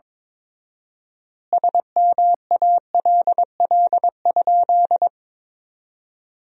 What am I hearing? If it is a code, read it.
Morse code, heard as a keyed tone, about 22 wpm, 700 Hz. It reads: SMALL?
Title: 4x Word-Spacing - Sentences from Top 200 Words - 22wpm